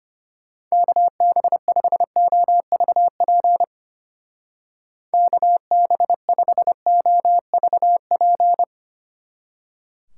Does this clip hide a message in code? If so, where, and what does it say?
Yes: Morse code from 0.7 to 8.6 s, heard as a keyed tone, about 32 wpm, 700 Hz. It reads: KB5OVP KB5OVP